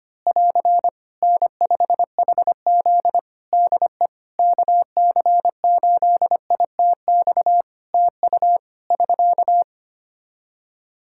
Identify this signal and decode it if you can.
Morse code, heard as a keyed tone, about 25 wpm, 700 Hz. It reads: <AR> N5HZ DE KC8ITX TU <SK>
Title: Practice QSO - Exchange 4 - Ending - 25wpm